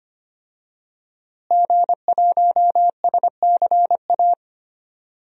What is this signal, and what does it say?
Morse code, heard as a keyed tone, about 25 wpm, 700 Hz. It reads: G1SCA
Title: DXCC-100 Call Signs - 5 Characters - Repeated 3x - 25wpm